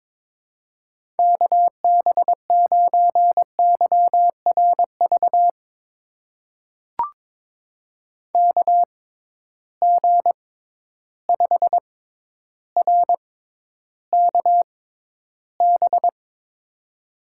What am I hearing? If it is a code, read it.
Morse code, heard as a keyed tone, about 22 wpm, spaced as for 11 wpm, 700 Hz. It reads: KB9YRV KG5RKB
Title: USA Call Signs - 6 Characters - 6x Character Spacing - 22wpm